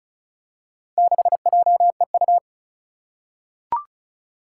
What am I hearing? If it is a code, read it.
Morse code, heard as a keyed tone, about 36 wpm, 690 Hz. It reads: 6JEU
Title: Four Challenging Letter-Number Combinations 35wpm